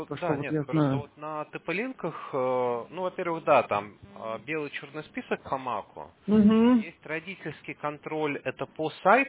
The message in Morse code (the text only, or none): none